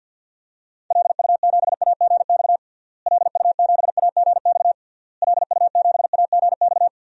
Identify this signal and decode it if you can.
Morse code, heard as a keyed tone, about 51 wpm, 700 Hz. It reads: LU7AGX LU7AGX LU7AGX
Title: DXCC-100 Call Signs - 6 Characters - Repeated 3x - 50wpm